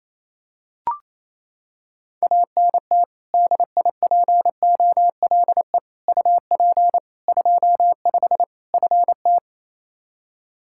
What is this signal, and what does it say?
Morse code, heard as a keyed tone, about 28 wpm, 700 Hz. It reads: ANT DIPOLE UP 25 FT
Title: Practice QSO - Exchange 2 - Info - 28wpm